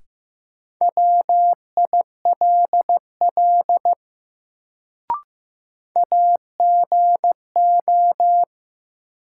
Morse code, heard as a keyed tone, about 15 wpm, 700 Hz. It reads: WILL AGO
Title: Sets of 2 Words - New Word Drill - 15wpm